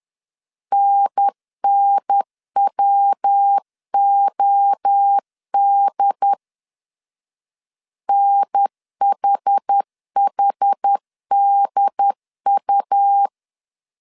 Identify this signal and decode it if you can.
Morse code, heard as a keyed tone, about 10 wpm, 800 Hz. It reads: NNWOD NHHDU
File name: CW / DET / 1 3 1500C S C 10wpm 3335